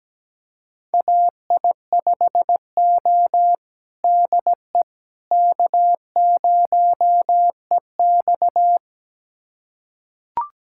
Morse code, heard as a keyed tone, about 17 wpm, 700 Hz. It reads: AI5O DE K0EX
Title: USA Callsign DE Callsign - 4 Characters - 17wpm